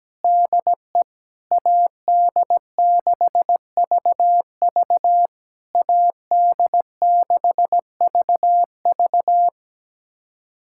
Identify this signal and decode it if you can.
Morse code, heard as a keyed tone, about 17 wpm, 700 Hz. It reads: DE AD6VV AD6VV
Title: SOTA QSO Practice 17wpm